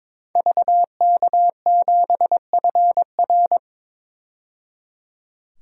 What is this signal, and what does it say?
Morse code, heard as a keyed tone, about 22 wpm, 700 Hz. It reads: VK7FR
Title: DXCC-100 Call Signs - 5 Characters - 22wpm